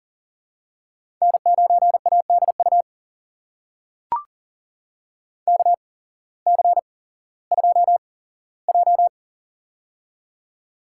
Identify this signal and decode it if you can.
Morse code, heard as a keyed tone, about 40 wpm, spaced as for 19 wpm, 700 Hz. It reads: N9ADU KC2J
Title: USA Call Signs - 8x Character Spacing - 40wpm